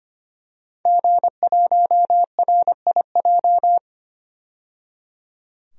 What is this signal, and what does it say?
Morse code, heard as a keyed tone, about 25 wpm, 700 Hz. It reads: G1RIJ
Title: DXCC-100 Call Signs - 5 Characters - 25wpm